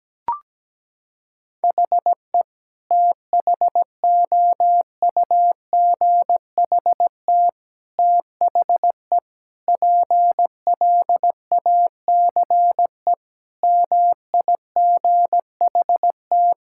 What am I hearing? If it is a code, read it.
Morse code, heard as a keyed tone, about 17 wpm, 700 Hz. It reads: HE THOUGHT THE PLACE MIGHT
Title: Sentences from Top 1600 Words 17wpm